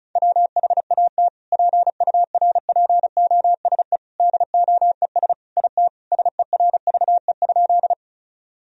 Morse code, heard as a keyed tone, about 34 wpm, 700 Hz. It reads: WHAT PURPOSE DOES IT SERVE?
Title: Sentences from Top 1000 Words - Review Part 2 - 35wpm